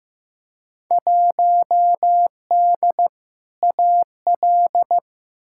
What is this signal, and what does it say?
Morse code, heard as a keyed tone, about 15 wpm, 700 Hz. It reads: 1D AL
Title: Field Day Exchanges - 15wpm